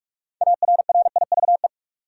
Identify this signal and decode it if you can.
Morse code, heard as a keyed tone, about 46 wpm, 700 Hz. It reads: ARRIVE
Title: Six Letter Words - 45wpm